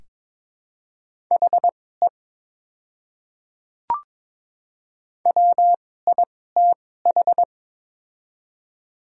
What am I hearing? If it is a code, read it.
Morse code, heard as a keyed tone, about 22 wpm, spaced as for 16 wpm, 700 Hz. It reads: HE WITH
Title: 2x Character Spacing - Top 100 Words - 22wpm